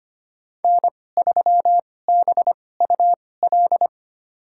Morse code, heard as a keyed tone, about 25 wpm, spaced as for 18 wpm, 700 Hz. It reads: N3BUL
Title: USA Call Signs - 2x Character Spacing - 25wpm